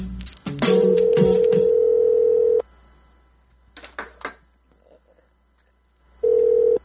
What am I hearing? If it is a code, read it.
Ringback tone (North America), 440 + 480 Hz.